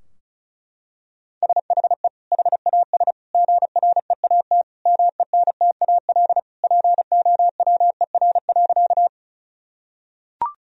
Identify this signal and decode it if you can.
Morse code, heard as a keyed tone, about 36 wpm, 700 Hz. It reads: SHE HAS GREAT MENTAL POWER.
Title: Sentences from Top 1000 Words - Review Part 4 - 35wpm